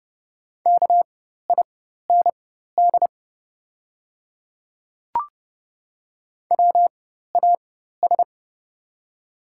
Morse code, heard as a keyed tone, about 29 wpm, spaced as for 14 wpm, 690 Hz. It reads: KIND WAS
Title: 4x Character Spacing - Top 300 Words - 30wpm